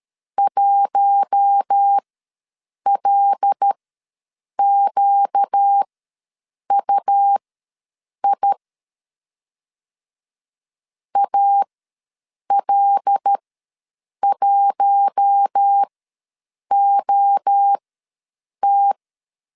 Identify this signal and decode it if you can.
Morse code, heard as a keyed tone, about 13 wpm, spaced as for 7 wpm, 800 Hz. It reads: 1LQUI AL1OT
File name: LEZIONI / 5 LBVC / 5 L ws 12wpm 2958